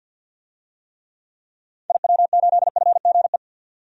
Morse code, heard as a keyed tone, about 50 wpm, 700 Hz. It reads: IW8FGE
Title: DXCC-100 Call Signs - 6 Characters - 50wpm